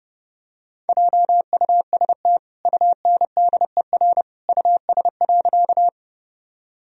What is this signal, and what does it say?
Morse code, heard as a keyed tone, about 30 wpm, 700 Hz. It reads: JUST UNDER US.
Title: Sentences from Top 1000 Words - Review Part 1 - 30wpm